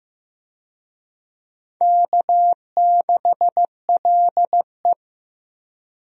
Morse code, heard as a keyed tone, about 15 wpm, 700 Hz. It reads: K6LE
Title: USA Call Signs - 4 Characters - 15wpm